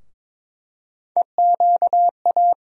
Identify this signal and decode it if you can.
Morse code, heard as a keyed tone, about 22 wpm, 700 Hz. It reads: EQA
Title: USA Call Sign Postfixes - 3 Letters 22wpm